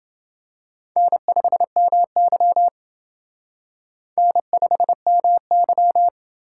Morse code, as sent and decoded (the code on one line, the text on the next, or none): -. ..... -- -.-- / -. ..... -- -.--
N5MY N5MY